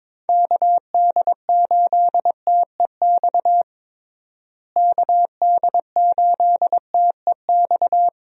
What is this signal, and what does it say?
Morse code, heard as a keyed tone, about 22 wpm, 700 Hz. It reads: KD8TEX KD8TEX
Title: USA Call Signs - 6 Characters - Repeated 3x - 22wpm